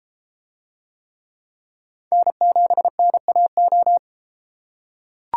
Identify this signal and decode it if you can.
Morse code, heard as a keyed tone, about 31 wpm, 700 Hz. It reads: N7NAO